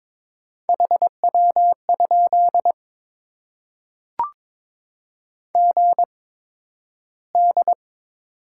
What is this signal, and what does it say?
Morse code, heard as a keyed tone, about 22 wpm, spaced as for 12 wpm, 700 Hz. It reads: HW? GD
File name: Top 90 QSO Elements - 8x Character Spacing - 22wpm